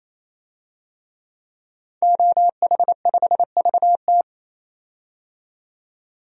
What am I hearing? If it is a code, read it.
Morse code, heard as a keyed tone, about 28 wpm, 700 Hz. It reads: OH5VT
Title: DXCC-100 Call Signs - 5 Characters - 28wpm